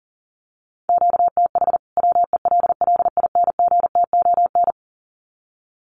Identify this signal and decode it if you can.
Morse code, heard as a keyed tone, about 40 wpm, 700 Hz. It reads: QTH WELLINGTON
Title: Common QSO Phrases 40wpm